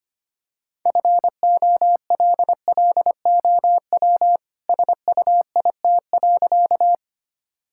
Morse code, heard as a keyed tone, about 25 wpm, 700 Hz. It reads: FOLLOW SUIT.
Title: Sentences from Top 1600 Words 25wpm